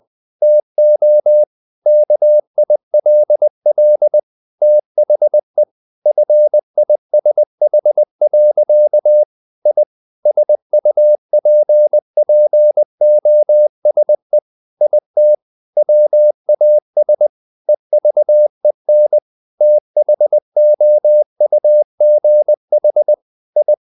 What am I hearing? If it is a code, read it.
Morse code, heard as a keyed tone, about 20 wpm, 600 Hz. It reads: TO KILL THE FISH. I SUPPOSE IT WAS EVEN THOUGH I
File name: the old man and the sea part 2 20wpm